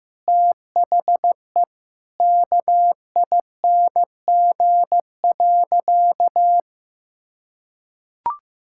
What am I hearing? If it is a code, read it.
Morse code, heard as a keyed tone, about 15 wpm, 700 Hz. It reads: THE KING.
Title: Sentences from Top 1000 Words - Review Part 6 - 15wpm